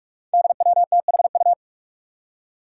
Morse code, heard as a keyed tone, about 46 wpm, 700 Hz. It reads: DWTHU
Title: Five Challenging Letter-Number Combinations 45wpm